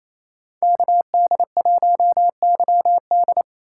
Morse code, heard as a keyed tone, about 28 wpm, 700 Hz. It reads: KD1YD
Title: USA Call Signs - 5 Characters - 4x Character Spacing - 28wpm